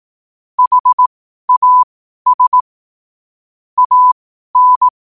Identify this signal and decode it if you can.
Morse code, heard as a keyed tone, about 18 wpm, spaced as for 12 wpm, 1000 Hz. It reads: HAS AN
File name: qotdfiles / 12wpm